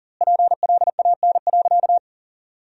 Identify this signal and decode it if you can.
Morse code, heard as a keyed tone, about 37 wpm, 700 Hz. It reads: PLAN.
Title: 2x Word-Spacing - Sentences from Top 500 Words - 40wpm